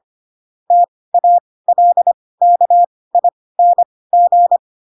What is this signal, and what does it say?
Morse code, heard as a keyed tone, about 25 wpm, spaced as for 18 wpm, 700 Hz. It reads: TALKING